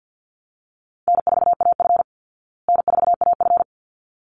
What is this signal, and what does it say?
Morse code, heard as a keyed tone, about 48 wpm, 700 Hz. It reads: N4AF N4AF
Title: Easy Call Signs Repeated 2x - 50wpm